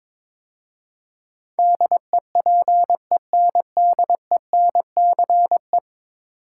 Morse code, heard as a keyed tone, about 22 wpm, 700 Hz. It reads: DEPENDENCE
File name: practice / 22wpm / 10 Letter Words - First 3 Letters Given - 22wpm